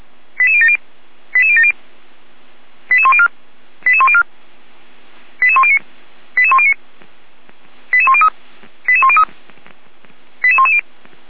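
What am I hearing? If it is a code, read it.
Five-tone selective call (ZVEI), heard as quick runs of five tones, tones of about 70 ms. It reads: 88070 88070 88104 88104 88109 88109 88103 88103 88110
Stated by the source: ZVEI five-tone selective call received by SDR.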